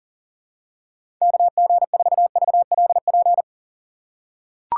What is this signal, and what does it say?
Morse code, heard as a keyed tone, about 40 wpm, 700 Hz. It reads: KG4VLP
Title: USA Call Signs - 6 Characters - 3x Character Spacing - 40wpm